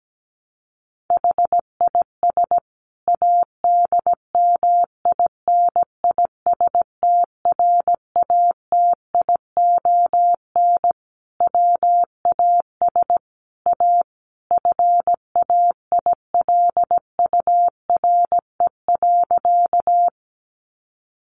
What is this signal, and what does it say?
Morse code, heard as a keyed tone, about 17 wpm, 700 Hz. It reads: HIS ADMINISTRATION WAS A FAILURE.
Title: Sentences from Top 1300 Words 17wpm